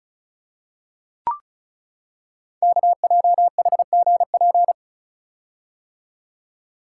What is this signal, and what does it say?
Morse code, heard as a keyed tone, about 36 wpm, 700 Hz. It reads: KJHGP
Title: Code Groups - 5 Letters - 35wpm